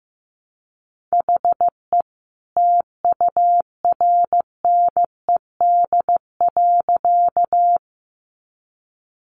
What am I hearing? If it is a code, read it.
Morse code, heard as a keyed tone, about 15 wpm, 700 Hz. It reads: HE TURNED.
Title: A Sound Of Thunder 15wpm